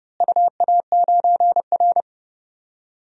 Morse code, heard as a keyed tone, about 32 wpm, 700 Hz. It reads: UA9R
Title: DXCC-100 Call Signs - 4 Characters - Repeated 3x - 30wpm